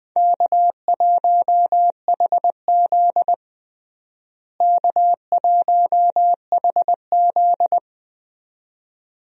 Morse code, heard as a keyed tone, about 20 wpm, 700 Hz. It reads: K1HZ K1HZ
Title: USA Call Signs - 4 Characters - Repeated 2x - 20wpm